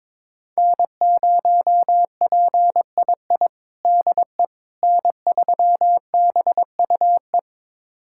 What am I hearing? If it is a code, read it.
Morse code, heard as a keyed tone, about 22 wpm, 700 Hz. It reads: N0PII DE N3BUE
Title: USA Callsign DE Callsign - 5 Characters - 22wpm